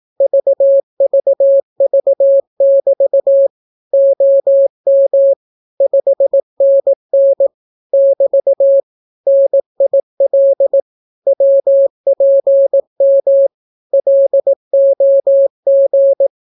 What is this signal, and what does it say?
Morse code, heard as a keyed tone, about 18 wpm, 550 Hz. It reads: VVV<BT> OM 5NN <BT> NIL WPM LOG
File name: naukapodstaw / skroty / skroty